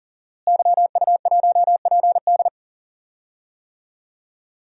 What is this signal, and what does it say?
Morse code, heard as a keyed tone, about 40 wpm, 700 Hz. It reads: YU1PD
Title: DXCC-100 Call Signs - 5 Characters - Repeated 2x - 40wpm